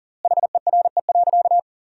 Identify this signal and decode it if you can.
Morse code, heard as a keyed tone, about 40 wpm, 700 Hz. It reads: HERE.
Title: 4x Word-Spacing - Sentences from Top 500 Words - 40wpm